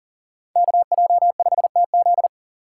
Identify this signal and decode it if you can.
Morse code, heard as a keyed tone, about 40 wpm, 700 Hz. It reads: KJ5TZ
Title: USA Call Signs - 5 Characters - 2x Character Spacing - 40wpm